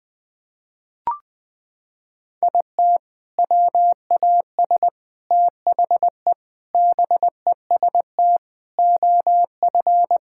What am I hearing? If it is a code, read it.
Morse code, heard as a keyed tone, about 21 wpm, 700 Hz. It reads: IT WAS THE BEST OF
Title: Sets of 8 Words - 20wpm